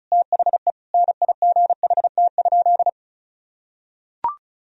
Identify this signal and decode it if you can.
Morse code, heard as a keyed tone, about 34 wpm, 700 Hz. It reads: THE NIGHT?
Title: Sentences from Top 1000 Words - Review Part 2 - 35wpm